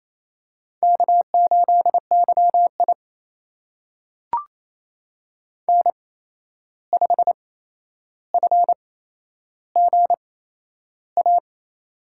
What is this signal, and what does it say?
Morse code, heard as a keyed tone, about 28 wpm, spaced as for 11 wpm, 700 Hz. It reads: K8YI N5FGA
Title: USA Call Signs - 8x Character Spacing - 28wpm